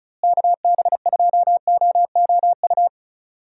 Morse code, heard as a keyed tone, about 36 wpm, 700 Hz. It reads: KB2OOU
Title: USA Call Signs - 6 Characters - Repeated 2x - 35wpm